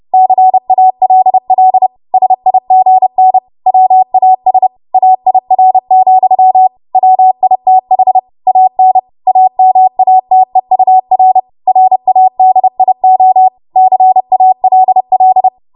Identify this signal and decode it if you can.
Morse code, heard as a keyed tone, about 30 wpm, 750 Hz. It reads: CALL SIGN WAS AIR, WITH AN AMATEUR RADIO CALL
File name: Archive / 30 WPM / 30WPM